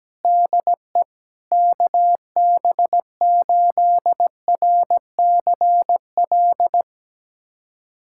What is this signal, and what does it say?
Morse code, heard as a keyed tone, about 17 wpm, 700 Hz. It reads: DE KB8RCL
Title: USA Callsign DE Callsign 17wpm